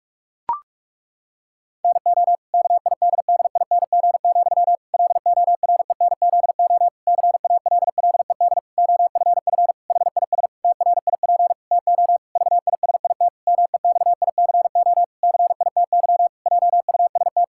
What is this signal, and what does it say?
Morse code, heard as a keyed tone, about 46 wpm, 700 Hz. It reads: NO KIDDING, LORENZO CALLED OFF HIS TRIP TO VISIT MEXICO CITY JUST
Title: Pangram Challenge v2 - 45wpm